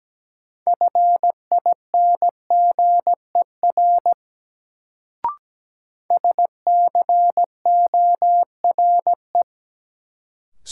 Morse code, heard as a keyed tone, about 17 wpm, 700 Hz. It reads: FINGER SCORE
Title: Sentences from Top 1600 Words - New Word Drill Alt - 17wpm